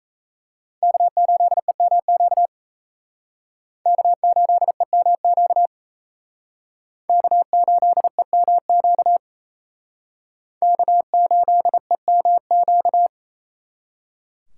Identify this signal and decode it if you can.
Morse code, heard as a keyed tone, about 35 wpm, 700 Hz. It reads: K8EMQ K8EMQ K8EMQ K8EMQ
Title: Speed Racing - USA 1x3 Call Signs 28wpm